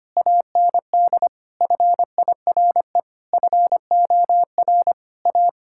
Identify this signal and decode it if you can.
Morse code, heard as a keyed tone, about 25 wpm, 700 Hz. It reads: AND FIRE FOR A
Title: Sentences from Top 1600 Words 25wpm